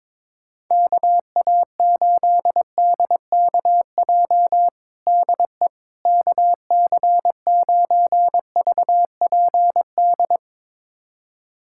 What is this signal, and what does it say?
Morse code, heard as a keyed tone, about 21 wpm, 700 Hz. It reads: KA8DKJ DE KC9VPD
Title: USA Callsign DE Callsign - 6 Characters - 22wpm